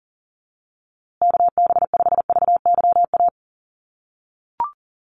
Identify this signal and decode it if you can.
Morse code, heard as a keyed tone, about 40 wpm, 700 Hz. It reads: KB5VYA
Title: USA Call Signs Repeated 2x 40wpm